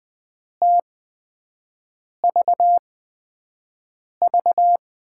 Morse code, heard as a keyed tone, about 20 wpm, spaced as for 5 wpm, 700 Hz. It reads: TVV
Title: USA Call Signs - 6 Characters - 8x Character Spacing - 20wpm